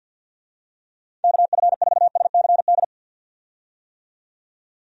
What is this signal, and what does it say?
Morse code, heard as a keyed tone, about 52 wpm, 700 Hz. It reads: KF4SCD
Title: USA Call Signs - 6 Characters - 50wpm